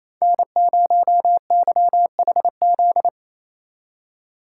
Morse code, heard as a keyed tone, about 36 wpm, 700 Hz. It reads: N0YHZ